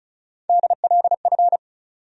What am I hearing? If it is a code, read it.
Morse code, heard as a keyed tone, about 36 wpm, 700 Hz. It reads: DLF
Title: USA Call Sign Postfixes - 3 Letters 35wpm